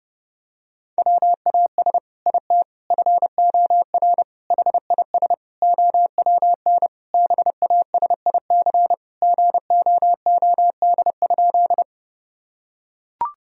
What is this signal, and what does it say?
Morse code, heard as a keyed tone, about 32 wpm, 700 Hz. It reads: WAS IT FOR HIS OWN BASIC GOOD?
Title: Sentences from Top 1000 Words - Review Part 3 - 30wpm